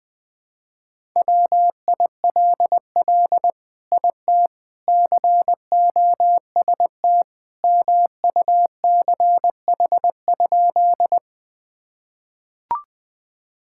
Morse code, heard as a keyed tone, about 21 wpm, 700 Hz. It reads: WILL IT COST MUCH?
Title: Sentences from Top 1000 Words - Review Part 5 - 20wpm